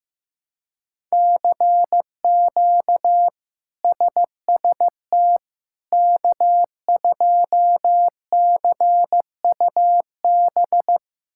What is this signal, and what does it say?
Morse code, heard as a keyed tone, about 15 wpm, 700 Hz. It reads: CQ SST K2CUB
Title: K1USN SST QSO Practice - 15wpm